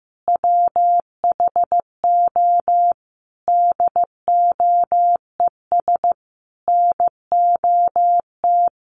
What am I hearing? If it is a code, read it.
Morse code, heard as a keyed tone, about 15 wpm, 700 Hz. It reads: WHO DOES NOT
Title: Sentences from Top 1000 Words 15wpm